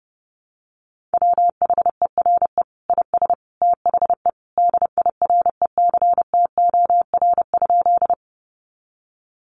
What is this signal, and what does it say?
Morse code, heard as a keyed tone, about 30 wpm, 700 Hz. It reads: WHERE IS THE DIRECTOR?
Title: Sentences from Top 1300 Words 30wpm